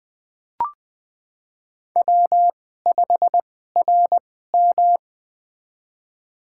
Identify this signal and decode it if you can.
Morse code, heard as a keyed tone, about 20 wpm, spaced as for 15 wpm, 700 Hz. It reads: W5RM
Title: USA Call Signs - 4 Characters - 2x Character Spacing - 20wpm